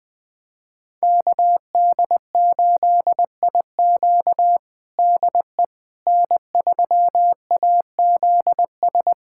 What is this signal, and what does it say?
Morse code, heard as a keyed tone, about 20 wpm, 700 Hz. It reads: KD8IQ DE N3AZS
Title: USA Callsign DE Callsign - 5 Characters - 20wpm